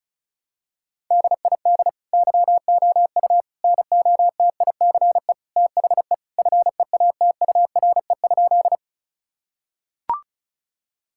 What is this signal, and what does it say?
Morse code, heard as a keyed tone, about 36 wpm, 700 Hz. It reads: DID YOU NOTICE THE FEATURE?
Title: Sentences from Top 1000 Words - Review Part 5 - 35wpm